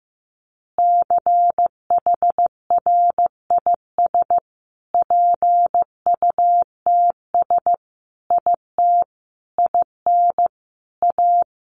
Morse code, heard as a keyed tone, about 15 wpm, 700 Hz. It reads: CHRIS PUTS IT IN A
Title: Additional Easy Phrases 15wpm